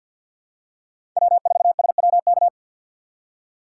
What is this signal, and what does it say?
Morse code, heard as a keyed tone, about 48 wpm, 700 Hz. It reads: W4SWK
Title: USA Call Signs - 5 Characters - 6x Character Spacing - 50wpm